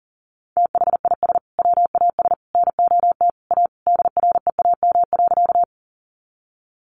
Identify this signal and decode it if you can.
Morse code, heard as a keyed tone, about 40 wpm, 700 Hz. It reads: THIS WAS NOT A DREAM.
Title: Sentences from Top 1500 Words 40wpm